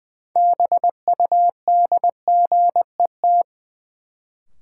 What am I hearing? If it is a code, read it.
Morse code, heard as a keyed tone, about 21 wpm, 700 Hz. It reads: BUDGET
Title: Sentences from Top 1200 Words - New Word Drill - 20wpm